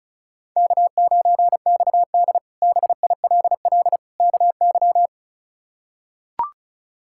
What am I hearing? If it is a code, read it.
Morse code, heard as a keyed tone, about 36 wpm, 700 Hz. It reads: K9XD BILL KY
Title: K1USN SST QSO Practice - 35wpm